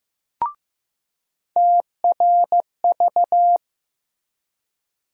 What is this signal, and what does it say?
Morse code, heard as a keyed tone, about 15 wpm, 700 Hz. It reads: TRV
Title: USA Call Sign Postfixes - 3 Letters 15wpm